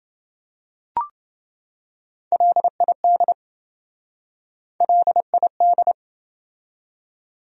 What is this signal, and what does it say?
Morse code, heard as a keyed tone, about 27 wpm, 700 Hz. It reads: LID LID